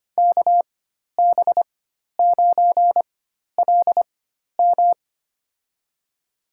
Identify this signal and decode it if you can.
Morse code, heard as a keyed tone, about 25 wpm, spaced as for 12 wpm, 700 Hz. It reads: KB9LM
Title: USA Call Signs - 5 Characters - 4x Character Spacing - 25wpm